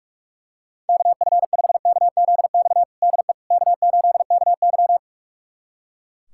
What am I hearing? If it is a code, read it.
Morse code, heard as a keyed tone, about 43 wpm, 700 Hz. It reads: KF5KZX DE K8KY
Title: USA Callsign DE Callsign 45wpm